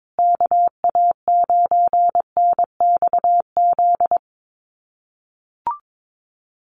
Morse code, heard as a keyed tone, about 22 wpm, 700 Hz. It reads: KA9NXZ
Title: USA Call Signs Repeated 2x 22wpm